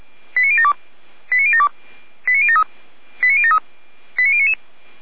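Five-tone selective call (ZVEI), heard as quick runs of five tones, tones of about 70 ms. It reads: 89072 89072 89073 89073 89099
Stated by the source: ZVEI five-tone selective call 89170 received on an SDR.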